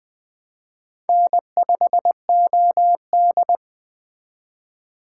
Morse code, heard as a keyed tone, about 20 wpm, 700 Hz. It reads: N5OD
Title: Speed Racing - USA Call Signs 17wpm